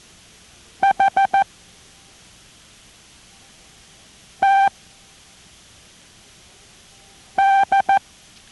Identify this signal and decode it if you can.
Morse code, heard as a keyed tone, about 14 wpm, spaced as for 3 wpm, 780 Hz. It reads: HTD